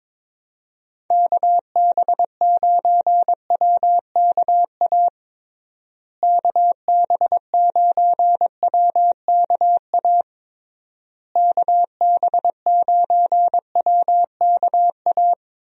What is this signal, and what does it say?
Morse code, heard as a keyed tone, about 22 wpm, 700 Hz. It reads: KB9WKA KB9WKA KB9WKA
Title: USA Call Signs - 6 Characters - Repeated 3x - 22wpm